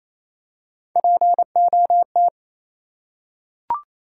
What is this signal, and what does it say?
Morse code, heard as a keyed tone, about 28 wpm, 700 Hz. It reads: POT
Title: Three Letter Words - 28wpm